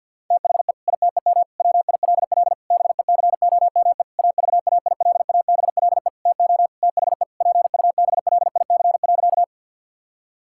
Morse code, heard as a keyed tone, about 51 wpm, 700 Hz. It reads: THE ITEM WILL BECOME AVAILABLE TO THE PUBLIC.
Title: Sentences from Top 1000 Words - Review Part 6 - 50wpm